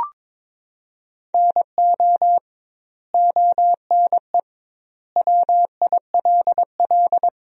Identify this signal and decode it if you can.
Morse code, heard as a keyed tone, about 22 wpm, 700 Hz. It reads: NO ONE WILL
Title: 2x Word-Spacing - Sentences from Top 100 Words - 22wpm